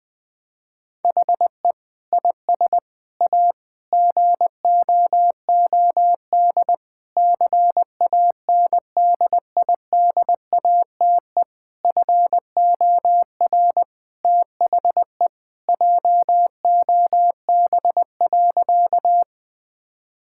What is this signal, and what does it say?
Morse code, heard as a keyed tone, about 21 wpm, 700 Hz. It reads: HE IS A GOOD CANDIDATE FOR THE JOB.
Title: Sentences from Top 1600 Words 20wpm